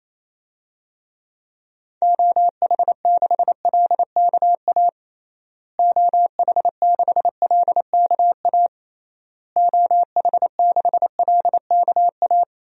Morse code, heard as a keyed tone, about 28 wpm, 700 Hz. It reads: OH6LKA OH6LKA OH6LKA
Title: DXCC-100 Call Signs - 6 Characters - Repeated 3x - 28wpm